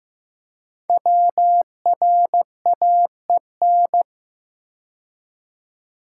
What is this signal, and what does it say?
Morse code, heard as a keyed tone, about 15 wpm, 700 Hz. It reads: WRAEN
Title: Code Groups - 5 Letters - 15wpm